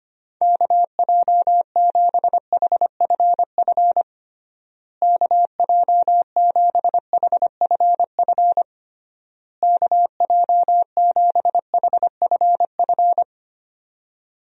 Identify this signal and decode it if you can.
Morse code, heard as a keyed tone, about 25 wpm, 700 Hz. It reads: KJ7HFF KJ7HFF KJ7HFF
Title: USA Call Signs - 6 Characters - Repeated 3x - 25wpm